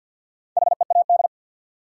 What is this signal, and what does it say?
Morse code, heard as a keyed tone, about 53 wpm, 700 Hz. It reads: HEAD
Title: Sentences from Top 300 Words - New Word Drill - 50wpm